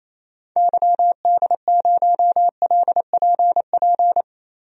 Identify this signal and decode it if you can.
Morse code, heard as a keyed tone, about 28 wpm, 700 Hz. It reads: YD0LPP
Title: DXCC-100 Call Signs - 6 Characters - Repeated 2x - 28wpm